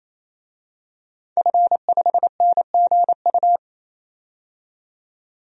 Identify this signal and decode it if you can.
Morse code, heard as a keyed tone, about 28 wpm, 700 Hz. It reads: F5NGU
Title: DXCC-100 Call Signs - 5 Characters - 28wpm